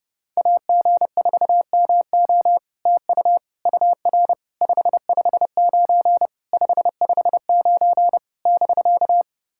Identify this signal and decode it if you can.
Morse code, heard as a keyed tone, about 28 wpm, 690 Hz. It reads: AG4MO TU UR 559 559 <BK>
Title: SOTA QSO Practice 30wpm